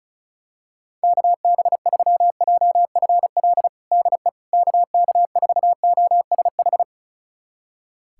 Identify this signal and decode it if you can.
Morse code, heard as a keyed tone, about 34 wpm, 700 Hz. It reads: KB3JFL DE KK4OSH
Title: USA Callsign DE Callsign 35wpm